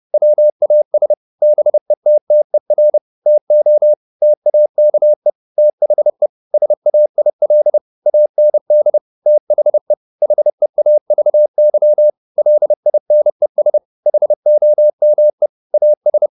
Morse code, heard as a keyed tone, about 30 wpm, 600 Hz. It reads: WAS BETTER TO TAKE THE SAIL AND THE HEAVY LINES HOME AS
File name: the old man and the sea / the old man and the sea 1 30wpm